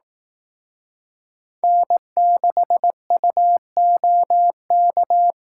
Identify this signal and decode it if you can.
Morse code, heard as a keyed tone, about 18 wpm, 700 Hz. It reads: N6UOK
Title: Speed Racing - USA 1x3 Call Signs 15wpm